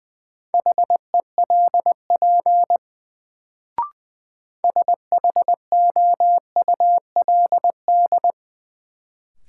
Morse code, heard as a keyed tone, about 20 wpm, 700 Hz. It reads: HELP SHOULD
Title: Sets of 2 Words - New Word Drill - 20wpm